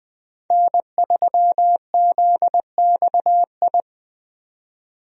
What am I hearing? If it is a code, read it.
Morse code, heard as a keyed tone, about 21 wpm, 700 Hz. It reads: N3ZXI
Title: USA Call Signs - 5 Characters - 20wpm